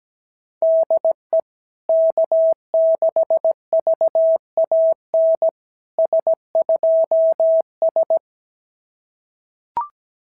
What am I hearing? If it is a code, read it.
Morse code, heard as a keyed tone, about 17 wpm, 650 Hz. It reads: DE K6VAN S2S
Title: SOTA QSO Practice 17wpm